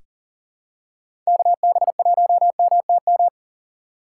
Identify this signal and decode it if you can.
Morse code, heard as a keyed tone, about 40 wpm, 700 Hz. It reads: KB1MTM
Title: USA Call Signs - 6 Characters - 6x Character Spacing - 40wpm